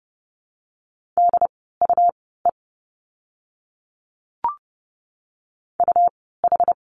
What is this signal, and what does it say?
Morse code, heard as a keyed tone, about 30 wpm, spaced as for 17 wpm, 700 Hz. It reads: DUE UH
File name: Lesson 17f - Letter u - Words 30wpm